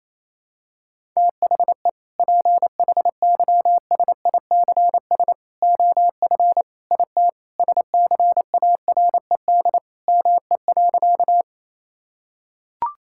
Morse code, heard as a keyed tone, about 28 wpm, 700 Hz. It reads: THE PHYSICS OF IT SCARED ME.
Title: Sentences from Top 1000 Words - Review Part 5 - 28wpm